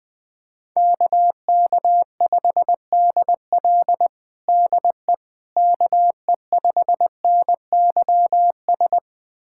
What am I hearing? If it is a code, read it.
Morse code, heard as a keyed tone, about 20 wpm, 700 Hz. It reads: KK5DL DE KE5NYS
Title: USA Callsign DE Callsign 20wpm